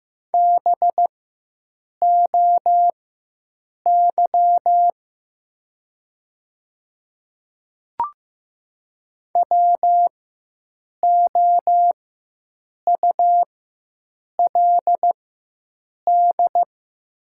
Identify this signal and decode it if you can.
Morse code, heard as a keyed tone, about 15 wpm, spaced as for 7 wpm, 700 Hz. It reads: BOY WOULD
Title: 4x Character Spacing - Top 300 Words - 15wpm